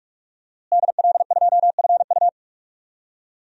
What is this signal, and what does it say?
Morse code, heard as a keyed tone, about 46 wpm, 700 Hz. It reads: DL2FU
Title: DXCC-100 Call Signs - 5 Characters - Repeated 2x - 45wpm